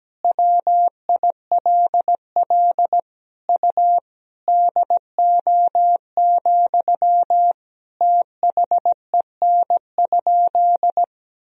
Morse code, heard as a keyed tone, about 17 wpm, 700 Hz. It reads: WILL U DO, THEN?
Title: Sentences from Top 100 Words and Abbreviations 17wpm